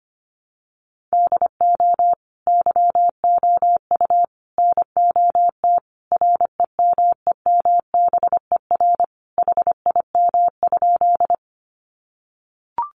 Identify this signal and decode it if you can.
Morse code, heard as a keyed tone, about 25 wpm, 700 Hz. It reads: DO YOU NOT REMEMBER HIM?
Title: Sentences from Top 600 Words 25wpm